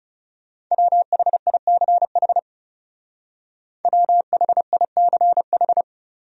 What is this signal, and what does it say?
Morse code, heard as a keyed tone, about 36 wpm, 700 Hz. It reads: WHICH WHICH
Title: Speed-Racing Top 100 Words 30wpm